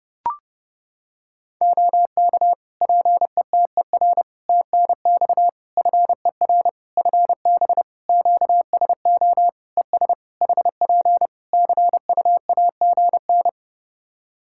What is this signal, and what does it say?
Morse code, heard as a keyed tone, about 30 wpm, 700 Hz. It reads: OK PETER TNX FER FB QSO ES HP CUAGN
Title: K1USN SST QSO Practice 30wpm